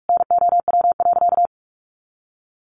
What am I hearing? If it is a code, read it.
Morse code, heard as a keyed tone, about 44 wpm, 700 Hz. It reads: NOW.
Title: Sentences from Top 1400 Words 45wpm